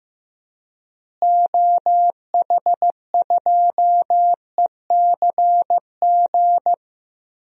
Morse code, heard as a keyed tone, about 15 wpm, 700 Hz. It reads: OH2ECG
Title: DXCC-100 Call Signs - 6 Characters - Repeated 3x - 15wpm